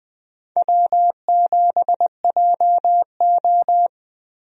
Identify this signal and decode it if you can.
Morse code, heard as a keyed tone, about 20 wpm, 700 Hz. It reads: W7JO
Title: USA Call Signs - 4 Characters - Repeated 2x - 20wpm